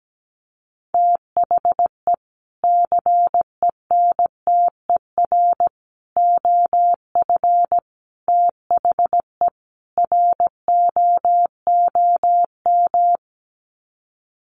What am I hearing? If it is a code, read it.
Morse code, heard as a keyed tone, about 17 wpm, 700 Hz. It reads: THE CENTER OF THE ROOM
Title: Sets of 5 Words Encore 17wpm